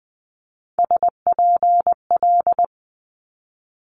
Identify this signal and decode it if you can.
Morse code, heard as a keyed tone, about 20 wpm, 700 Hz. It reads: SPL
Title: Multi-letter Phonemes Quick Fire 20wpm